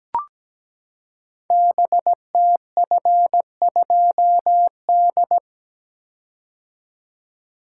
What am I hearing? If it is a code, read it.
Morse code, heard as a keyed tone, about 17 wpm, 700 Hz. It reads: BTF2D
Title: Code Groups - 5 Letter:Numbers - 17wpm